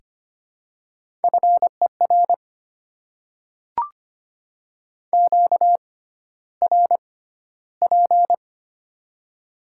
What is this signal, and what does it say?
Morse code, heard as a keyed tone, about 25 wpm, spaced as for 13 wpm, 700 Hz. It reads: FER QRP
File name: Top 90 QSO Elements - 6x Character Spacing - 25wpm